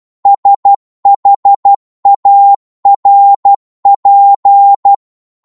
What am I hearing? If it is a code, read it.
Morse code, heard as a keyed tone, about 12 wpm, 800 Hz. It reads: SHARP